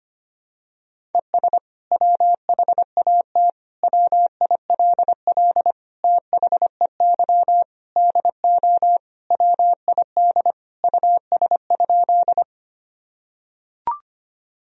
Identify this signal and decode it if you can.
Morse code, heard as a keyed tone, about 25 wpm, 700 Hz. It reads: ES WHAT WILL THEY DO WID US?
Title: Sentences from Top 200 Words and Abbreviations 25wpm